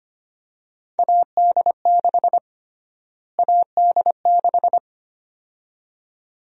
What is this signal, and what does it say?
Morse code, heard as a keyed tone, about 25 wpm, 700 Hz. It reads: AD6 AD6
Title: USA Call Sign Prefixes - Repeated 2x - 25wpm